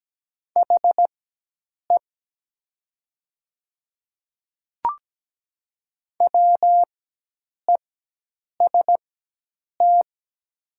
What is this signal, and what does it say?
Morse code, heard as a keyed tone, about 17 wpm, spaced as for 8 wpm, 700 Hz. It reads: HE WEST
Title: 4x Character Spacing - Top 500 Words - 17wpm